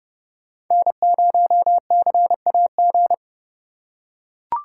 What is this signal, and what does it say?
Morse code, heard as a keyed tone, about 30 wpm, 700 Hz. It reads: N0CAG
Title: USA Call Signs - 5 Characters - 3x Character Spacing - 30wpm